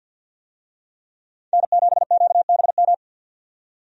Morse code, heard as a keyed tone, about 52 wpm, 700 Hz. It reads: N7QBM
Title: USA Call Signs - 5 Characters - 50wpm